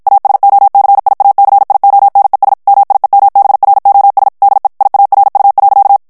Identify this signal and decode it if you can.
Morse code, heard as a keyed tone, about 53 wpm, 800 Hz. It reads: ASOCIACIONES MIEMBROS DE IARU.
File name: ESPAGNOL TEXTE EN CLAIR 1 VITESSE 50 MOTS MINUTES